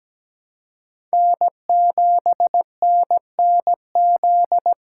Morse code, heard as a keyed tone, about 17 wpm, 700 Hz. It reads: N7NNZ
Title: USA Call Signs - 5 Characters - 4x Character Spacing - 17wpm